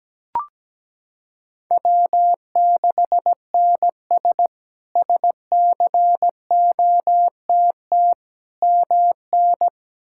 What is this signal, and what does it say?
Morse code, heard as a keyed tone, about 17 wpm, 700 Hz. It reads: W6NS SCOTT MN
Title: K1USN SST QSO Practice - 17wpm